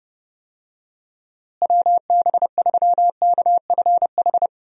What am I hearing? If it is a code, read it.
Morse code, heard as a keyed tone, about 30 wpm, 700 Hz. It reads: WB3KFH
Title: USA Call Signs - 6 Characters - Repeated 2x - 30wpm